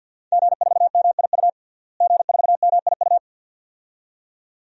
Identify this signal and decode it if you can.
Morse code, heard as a keyed tone, about 52 wpm, 700 Hz. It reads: G4MIU G4MIU
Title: DXCC-100 Call Signs - 5 Characters - Repeated 2x - 50wpm